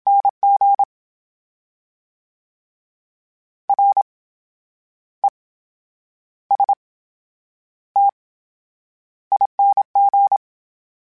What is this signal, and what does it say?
Morse code, heard as a keyed tone, about 27 wpm, spaced as for 10 wpm, 800 Hz. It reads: NG RESTING
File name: Special INGs / Special INGs-mp3s / VerySlowToFaster Ing @25